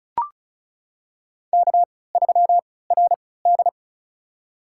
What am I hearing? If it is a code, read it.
Morse code, heard as a keyed tone, about 36 wpm, spaced as for 20 wpm, 700 Hz. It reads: K3RD
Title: USA Call Signs - 4 Characters - 3x Character Spacing - 35wpm